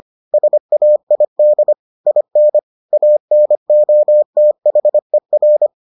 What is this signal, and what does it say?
Morse code, heard as a keyed tone, about 25 wpm, 600 Hz. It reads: SAID IN ANOTHER